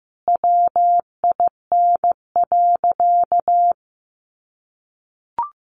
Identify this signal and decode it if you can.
Morse code, heard as a keyed tone, about 15 wpm, 700 Hz. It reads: WIN.
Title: Sentences from Top 1500 Words 15wpm